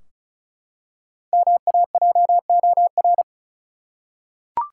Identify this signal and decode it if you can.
Morse code, heard as a keyed tone, about 36 wpm, 700 Hz. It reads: MAJOR
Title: Five Letter Words - 35wpm